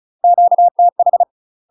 Morse code, heard as a keyed tone, about 36 wpm, 700 Hz. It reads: QTH
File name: UK MASTER QSOs TRAINING / 35WPM / UK MASTER QSOs 0012